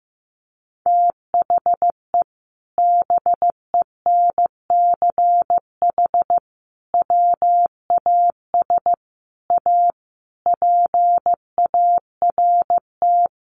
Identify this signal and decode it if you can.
Morse code, heard as a keyed tone, about 15 wpm, 700 Hz. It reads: THE BENCH WAS A PART
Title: Sentences from Top 1400 Words 15wpm